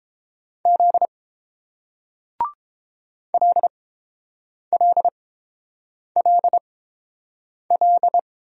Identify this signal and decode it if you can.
Morse code, heard as a keyed tone, about 31 wpm, spaced as for 7 wpm, 700 Hz. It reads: Z LLLL